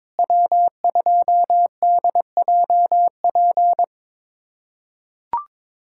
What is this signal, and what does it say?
Morse code, heard as a keyed tone, about 22 wpm, 700 Hz. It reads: W2DJP
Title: USA Callsign DE Callsign - 5 Characters - 22wpm